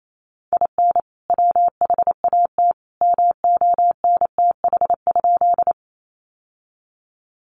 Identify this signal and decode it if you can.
Morse code, heard as a keyed tone, about 28 wpm, 700 Hz. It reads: IN WHAT MONTH?
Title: Sentences from Top 800 Words 28wpm